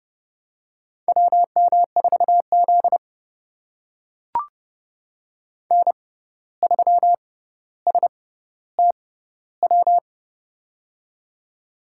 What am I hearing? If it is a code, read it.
Morse code, heard as a keyed tone, about 30 wpm, spaced as for 14 wpm, 700 Hz. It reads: WM4Z N3STW
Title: USA Call Signs - 6x Character Spacing - 30wpm